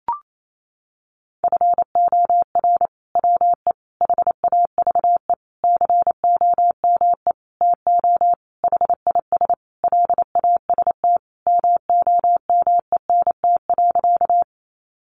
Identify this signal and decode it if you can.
Morse code, heard as a keyed tone, about 28 wpm, 700 Hz. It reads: FOR WE HAVE COME TO HIS LAST MOMENT.
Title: Sentences from Top 1000 Words 28wpm